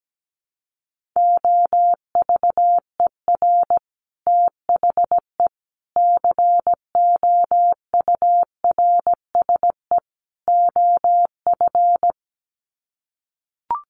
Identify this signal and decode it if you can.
Morse code, heard as a keyed tone, about 17 wpm, 700 Hz. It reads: OVER THE COURSE OF
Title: Sets of 4 Words 17wpm